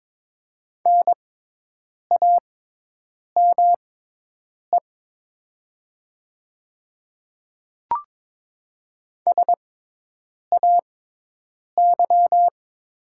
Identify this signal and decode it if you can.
Morse code, heard as a keyed tone, about 22 wpm, spaced as for 8 wpm, 700 Hz. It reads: NAME SAY
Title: 6x Character Spacing - Top 200 Words - 22wpm